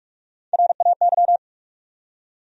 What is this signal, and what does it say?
Morse code, heard as a keyed tone, about 44 wpm, 700 Hz. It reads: RAY
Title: USA Common Names - Max 3 Letters - 45wpm